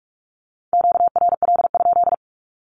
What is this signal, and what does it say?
Morse code, heard as a keyed tone, about 44 wpm, 700 Hz. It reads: QRL?
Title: Q-Codes Repeated 3x 45wpm